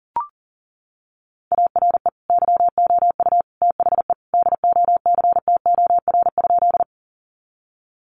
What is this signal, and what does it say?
Morse code, heard as a keyed tone, about 40 wpm, 700 Hz. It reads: ARE YOU THE DOCTOR?
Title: Sentences from Top 1500 Words 40wpm